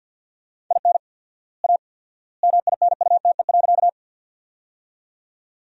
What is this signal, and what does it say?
Morse code, heard as a keyed tone, about 52 wpm, 700 Hz. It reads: IN A MINUTE.
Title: 4x Word-Spacing - Sentences from Top 500 Words - 50wpm